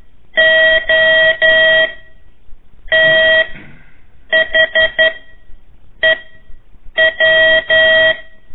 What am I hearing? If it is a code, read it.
Morse code, heard as a keyed tone, about 11 wpm, 1900 Hz. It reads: O T H E W
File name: 3 lezione CW 1 AIR